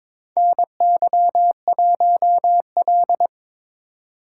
Morse code, heard as a keyed tone, about 22 wpm, 700 Hz. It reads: NY1L
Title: USA Call Signs - 4 Characters - 2x Character Spacing - 22wpm